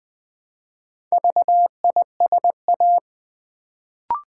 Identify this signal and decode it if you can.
Morse code, heard as a keyed tone, about 20 wpm, 700 Hz. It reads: VISA
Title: Build A Word - 1 Letter At A Time - 4-Letter Words - 20wpm